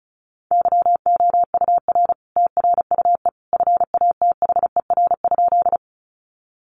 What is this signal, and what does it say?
Morse code, heard as a keyed tone, about 34 wpm, 700 Hz. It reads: YOUR TRUE FATHER?
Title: Sentences from Top 400 Words 35wpm